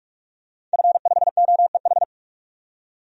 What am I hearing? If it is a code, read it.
Morse code, heard as a keyed tone, about 46 wpm, 700 Hz. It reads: F5OEH
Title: DXCC-100 Call Signs - 5 Characters - 45wpm